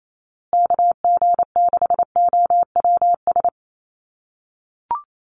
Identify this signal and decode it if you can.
Morse code, heard as a keyed tone, about 28 wpm, 700 Hz. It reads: KG6OWS
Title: USA Call Signs Repeated 3x 28wpm